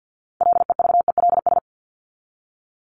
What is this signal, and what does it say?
Morse code, heard as a keyed tone, about 48 wpm, 700 Hz. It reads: LEVELS
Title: ES:S Suffix - Top 100 Words with Base Words 50wpm